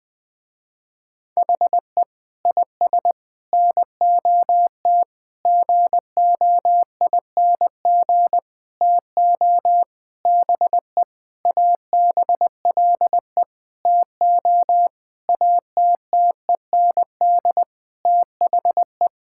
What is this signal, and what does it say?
Morse code, heard as a keyed tone, about 20 wpm, 700 Hz. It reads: HE IS NOT GOING TO BE ABLE TO ATTEND THE
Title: Sentences with 6-Word Phrases - Spoken Beginning - 20wpm